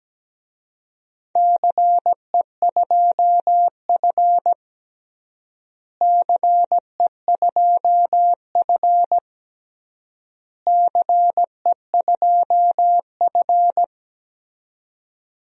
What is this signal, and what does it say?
Morse code, heard as a keyed tone, about 17 wpm, 700 Hz. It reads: CE2F CE2F CE2F
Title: DXCC-100 Call Signs - 4 Characters - Repeated 3x - 17wpm